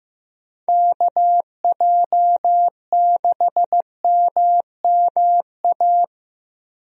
Morse code, heard as a keyed tone, about 15 wpm, 700 Hz. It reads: KJ6MMA
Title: USA Call Signs - 6 Characters - Repeated 3x - 15wpm